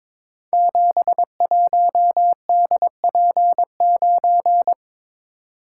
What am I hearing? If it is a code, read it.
Morse code, heard as a keyed tone, about 23 wpm, 700 Hz. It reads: 71DP9
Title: Code Groups - 5 Letter:Numbers - 22wpm